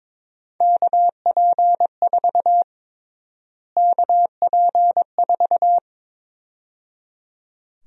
Morse code, heard as a keyed tone, about 22 wpm, 700 Hz. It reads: KP4 KP4
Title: USA Call Sign Prefixes - Repeated 2x - 22wpm